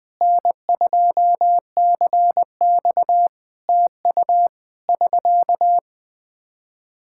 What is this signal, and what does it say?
Morse code, heard as a keyed tone, about 19 wpm, 700 Hz. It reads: N2CX TU <SK>
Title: Practice QSO - Exchange 4 - Ending - 20wpm